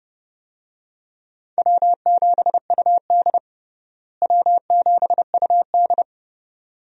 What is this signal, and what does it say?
Morse code, heard as a keyed tone, about 31 wpm, 700 Hz. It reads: W7UD W7UD
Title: USA Call Signs - 4 Characters - Repeated 2x - 30wpm